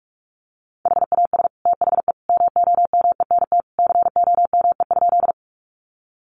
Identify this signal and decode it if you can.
Morse code, heard as a keyed tone, about 46 wpm, 700 Hz. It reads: HAS THE MOMENT COME?
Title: Sentences from Top 1000 Words 45wpm